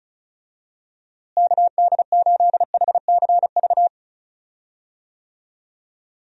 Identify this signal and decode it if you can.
Morse code, heard as a keyed tone, about 34 wpm, 700 Hz. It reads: KD8HCV
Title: USA Call Signs - 6 Characters - Repeated 2x - 35wpm